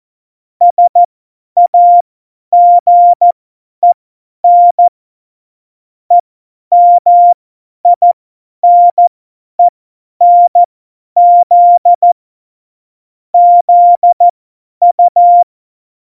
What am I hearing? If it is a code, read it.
Morse code, heard as a keyed tone, about 14 wpm, spaced as for 10 wpm, 700 Hz. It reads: SAGEN EMINENZ ZU
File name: Morsekurs / Witze / Witz 27. Oktober 2020, 50 F70